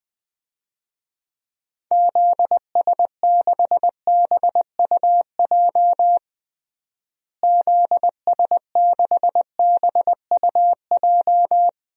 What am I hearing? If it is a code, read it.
Morse code, heard as a keyed tone, about 20 wpm, 700 Hz. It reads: ZS6BUJ ZS6BUJ
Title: DXCC-100 Call Signs - 6 Characters - Repeated 3x - 20wpm